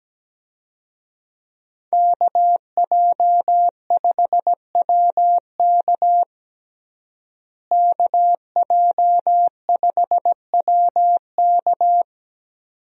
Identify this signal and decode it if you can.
Morse code, heard as a keyed tone, about 17 wpm, 700 Hz. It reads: KJ5WK KJ5WK
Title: USA Call Signs - 5 Characters - Repeated 2x - 17wpm